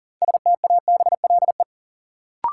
Morse code, heard as a keyed tone, about 40 wpm, 700 Hz. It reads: STABLE
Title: Sentences from Top 1500 Words - New Word Drill Alt - 40wpm